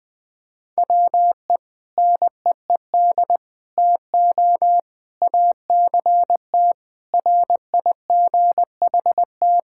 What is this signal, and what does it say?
Morse code, heard as a keyed tone, about 21 wpm, 700 Hz. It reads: WE NEED TO ACT RIGHT
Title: Sentences with 2-Word Phrases - 20wpm